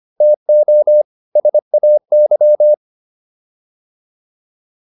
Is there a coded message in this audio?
Yes: Morse code from 0.2 to 2.7 s, heard as a keyed tone, about 25 wpm, 600 Hz. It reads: TO SAY